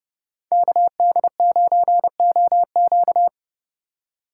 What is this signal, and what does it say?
Morse code, heard as a keyed tone, about 30 wpm, 700 Hz. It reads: KD9OQ